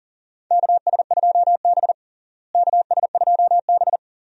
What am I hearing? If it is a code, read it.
Morse code, heard as a keyed tone, about 40 wpm, 700 Hz. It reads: KS2B KS2B
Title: USA Call Signs - 4 Characters - Repeated 3x - 40wpm